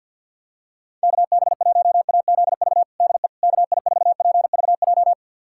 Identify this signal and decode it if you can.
Morse code, heard as a keyed tone, about 51 wpm, 700 Hz. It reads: KB1AZV DE KI4PVJ
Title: USA Callsign DE Callsign - 6 Characters - 50wpm